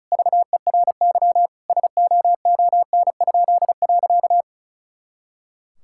Morse code, heard as a keyed tone, about 36 wpm, 700 Hz. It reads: VERY SOON?.
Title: Pangram Challenge v2 - 35wpm